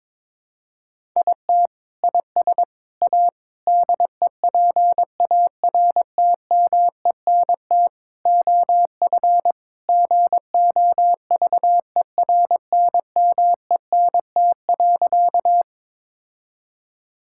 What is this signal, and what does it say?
Morse code, heard as a keyed tone, about 22 wpm, 700 Hz. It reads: IT IS A DEPARTMENT OF GOVERNMENT.
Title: Sentences from Top 1000 Words - Review Part 5 - 22wpm